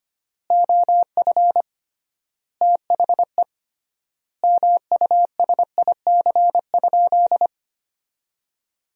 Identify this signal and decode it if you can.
Morse code, heard as a keyed tone, about 25 wpm, 700 Hz. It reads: OF THE MUSIC?
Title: 3x Word-Spacing - Sentences from Top 300 Words - 25wpm